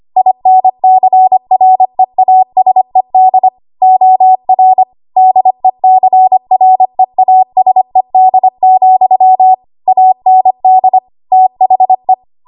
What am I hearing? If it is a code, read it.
Morse code, heard as a keyed tone, about 25 wpm, 750 Hz. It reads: INCREASED OR DECREASED, AND THE
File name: Morse / 25WPM